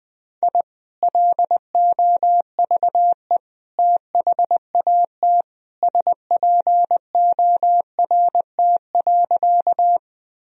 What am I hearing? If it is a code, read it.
Morse code, heard as a keyed tone, about 20 wpm, 700 Hz. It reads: I LOVE THAT SPORT.
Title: Sentences from Top 1000 Words - Review Part 4 - 20wpm